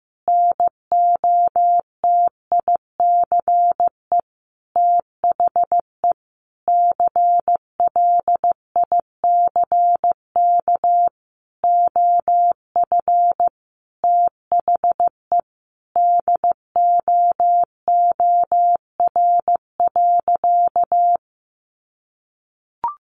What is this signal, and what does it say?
Morse code, heard as a keyed tone, about 15 wpm, 700 Hz. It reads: NOTICE THE CLICK OF THE DOOR.
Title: Sentences from Top 1100 Words 15wpm